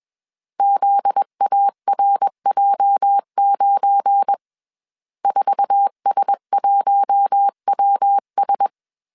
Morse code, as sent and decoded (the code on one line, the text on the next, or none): --.. .- .-. .--- ----. / ....- ... .---- .-- ...
ZARJ9 4S1WS